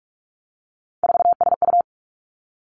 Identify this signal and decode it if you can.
Morse code, heard as a keyed tone, about 46 wpm, 700 Hz. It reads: 4SU
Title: Three Challenging Letter-Number Combinations 45wpm